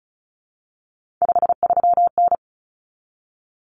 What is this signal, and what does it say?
Morse code, heard as a keyed tone, about 34 wpm, 700 Hz. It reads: 53N
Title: Common Cut Numbers - 3-Digit 35wpm